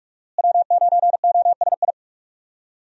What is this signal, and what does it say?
Morse code, heard as a keyed tone, about 41 wpm, 700 Hz. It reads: W9OSI
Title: USA Call Signs - 5 Characters - Repeated 3x - 45wpm